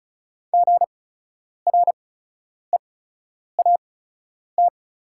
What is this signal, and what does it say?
Morse code, heard as a keyed tone, about 36 wpm, spaced as for 10 wpm, 700 Hz. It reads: GREAT
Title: 8x Character Spacing - Top 300 Words - 35wpm